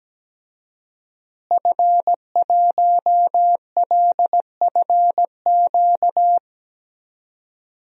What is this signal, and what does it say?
Morse code, heard as a keyed tone, about 17 wpm, 700 Hz. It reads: F1LFQ
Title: DXCC-100 Call Signs - 5 Characters - Repeated 3x - 17wpm